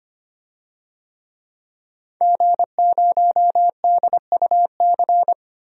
Morse code, heard as a keyed tone, about 25 wpm, 700 Hz. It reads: G0DUC
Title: DXCC-100 Call Signs - 5 Characters - 25wpm